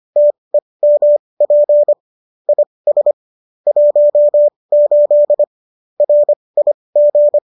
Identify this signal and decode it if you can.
Morse code, heard as a keyed tone, about 25 wpm, spaced as for 20 wpm, 590 Hz. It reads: TEMP IS 18 RIG